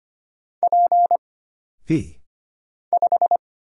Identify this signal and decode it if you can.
Morse code, heard as a keyed tone, about 25 wpm, 700 Hz. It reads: P 5
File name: Single Character (Letter:Number:Punctuation:Prosign) - Mind-Melt - 25wpm